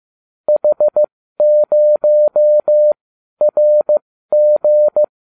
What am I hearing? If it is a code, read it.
Morse code, heard as a keyed tone, about 15 wpm, spaced as for 12 wpm, 600 Hz. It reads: H0RG